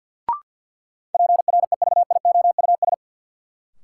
Morse code, heard as a keyed tone, about 53 wpm, 700 Hz. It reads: PREVIOUS
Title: Sentences from Top 1500 Words - New Word Drill - 50wpm